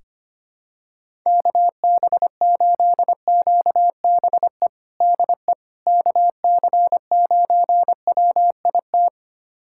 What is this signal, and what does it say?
Morse code, heard as a keyed tone, about 25 wpm, 700 Hz. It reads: KB8QBE DE KC9WIT
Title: USA Callsign DE Callsign 25wpm